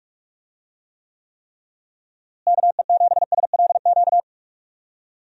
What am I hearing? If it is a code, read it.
Morse code, heard as a keyed tone, about 46 wpm, 700 Hz. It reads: KE7SLQ
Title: USA Call Signs - 6 Characters - Repeated 3x - 45wpm